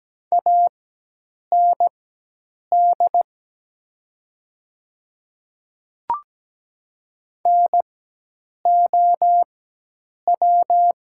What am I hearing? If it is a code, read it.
Morse code, heard as a keyed tone, about 17 wpm, spaced as for 8 wpm, 700 Hz. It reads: AND NOW
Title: 4x Character Spacing - Top 100 Words - 17wpm